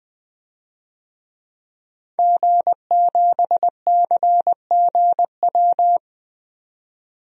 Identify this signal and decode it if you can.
Morse code, heard as a keyed tone, about 20 wpm, 700 Hz. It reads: G7CGW
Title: DXCC-100 Call Signs - 5 Characters - 20wpm